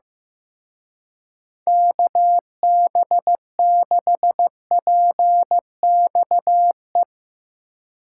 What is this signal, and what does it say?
Morse code, heard as a keyed tone, about 15 wpm, 700 Hz. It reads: KB6PXE
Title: Speed Racing - USA Call Signs 15wpm